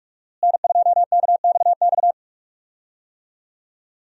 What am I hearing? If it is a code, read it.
Morse code, heard as a keyed tone, about 46 wpm, 700 Hz. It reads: N2KXX
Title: USA Call Signs - 5 Characters - Repeated 3x - 45wpm